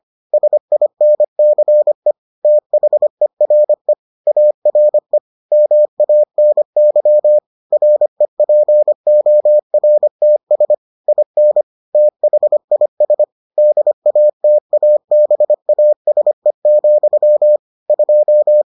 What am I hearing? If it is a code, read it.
Morse code, heard as a keyed tone, about 25 wpm, 600 Hz. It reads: SINCE THERE ARE MANY REPORTS IN THIS DATABASE, 2